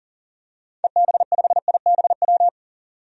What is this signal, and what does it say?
Morse code, heard as a keyed tone, about 40 wpm, 700 Hz. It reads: EB5IBW
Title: DXCC-100 Call Signs - 6 Characters - 40wpm